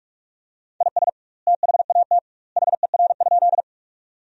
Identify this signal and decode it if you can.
Morse code, heard as a keyed tone, about 45 wpm, 700 Hz. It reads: IS THAT HER?
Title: 2x Word-Spacing - Sentences from Top 100 Words - 45wpm